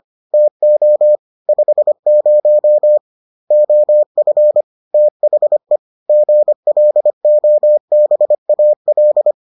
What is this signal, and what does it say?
Morse code, heard as a keyed tone, about 25 wpm, 600 Hz. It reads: TO 50 OF THE GLOBAL